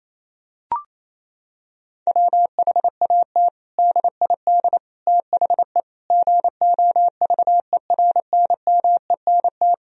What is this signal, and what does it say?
Morse code, heard as a keyed tone, about 28 wpm, 700 Hz. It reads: WHAT DID THE GOVERNMENT
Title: Sentences from Top 1000 Words - Review Part 3 - 28wpm